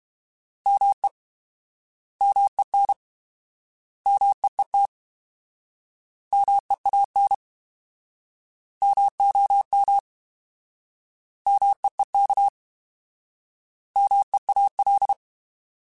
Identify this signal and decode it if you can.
Morse code, heard as a keyed tone, about 32 wpm, 800 Hz. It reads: ME MEN MEET MEAN MOM MEEK MEAL
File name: Alphabetical / Alphabetical mp3s / Ms @30